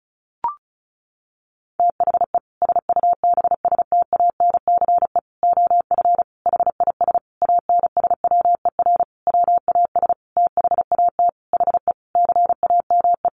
Morse code, heard as a keyed tone, about 34 wpm, 700 Hz. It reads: THE SUBSTANCE OF HIS ANSWER WAS THAT HE CAME
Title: Sentences from Top 1500 Words 35wpm